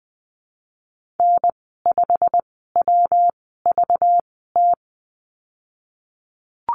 Morse code, heard as a keyed tone, about 20 wpm, spaced as for 14 wpm, 700 Hz. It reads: N5WVT
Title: 193 Lesson 31g - Letter v - Call Signs 20wpm